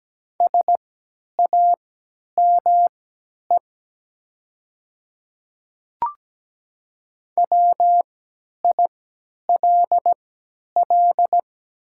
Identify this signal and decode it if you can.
Morse code, heard as a keyed tone, about 17 wpm, spaced as for 10 wpm, 700 Hz. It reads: SAME WILL
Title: 3x Character Spacing - Top 300 Words - 17wpm